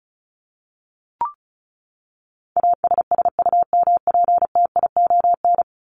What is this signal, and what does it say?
Morse code, heard as a keyed tone, about 34 wpm, 700 Hz. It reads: ASSUMPTION
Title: Top 1751-2000 Words 35wpm